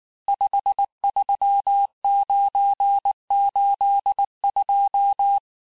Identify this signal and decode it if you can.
Morse code, heard as a keyed tone, about 19 wpm, 800 Hz. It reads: 53982
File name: cours-du-lundi files / cours-1589-texte-chiffre-vitesse-18mpm